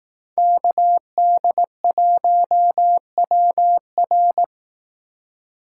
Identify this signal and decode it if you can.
Morse code, heard as a keyed tone, about 18 wpm, 700 Hz. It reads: KD1WR
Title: Speed Racing - USA 2x2 Call Signs 15wpm